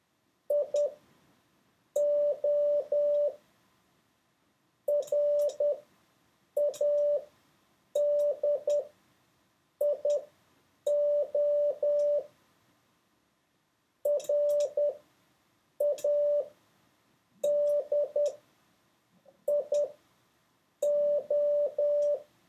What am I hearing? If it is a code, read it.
Morse code, heard as a keyed tone, about 10 wpm, spaced as for 6 wpm, 580 Hz. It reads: IO RADIO RADIO